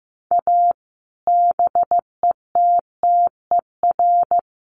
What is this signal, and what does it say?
Morse code, heard as a keyed tone, about 15 wpm, 700 Hz. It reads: A BETTER
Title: Sets of 5 Words Encore2 15wpm